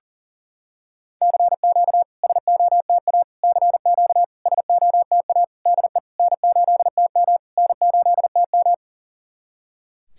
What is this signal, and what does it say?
Morse code, heard as a keyed tone, about 39 wpm, 690 Hz. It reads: CQ SOTA CQ SOTA DE N8TM N8TM
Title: SOTA QSO Practice 40wpm